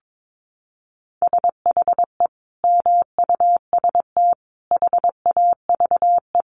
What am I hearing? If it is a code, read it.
Morse code, heard as a keyed tone, about 22 wpm, 700 Hz. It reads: SHE MUST HAVE
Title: Sentences from Top 1100 Words 22wpm